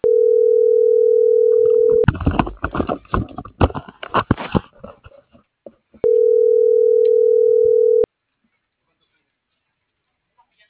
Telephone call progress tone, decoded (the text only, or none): ringback tone